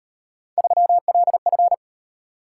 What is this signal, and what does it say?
Morse code, heard as a keyed tone, about 38 wpm, 700 Hz. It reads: 3LF